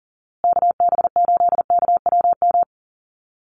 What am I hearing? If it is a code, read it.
Morse code, heard as a keyed tone, about 37 wpm, 700 Hz. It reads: KB8KWM
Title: USA Call Signs Repeated 3x 40wpm